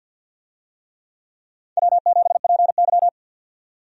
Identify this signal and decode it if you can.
Morse code, heard as a keyed tone, about 52 wpm, 700 Hz. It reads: W7PY
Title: USA Call Signs - 4 Characters - 50wpm